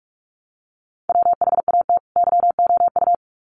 Morse code, heard as a keyed tone, about 44 wpm, 700 Hz. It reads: WHAT YOU
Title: Sets of 2 Words 45wpm